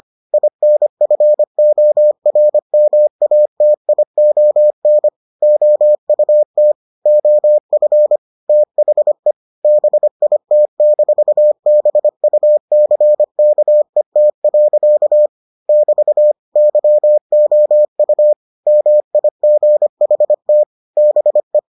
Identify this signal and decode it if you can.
Morse code, heard as a keyed tone, about 25 wpm, 600 Hz. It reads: INFORMATION OUT OF THE BIT-BUCKET. <BT> YOU MIGHT BE